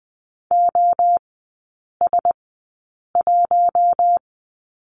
Morse code, heard as a keyed tone, about 20 wpm, spaced as for 8 wpm, 690 Hz. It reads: OS1
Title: 060 Lesson 11e - Letter h - Three Characters 20wpm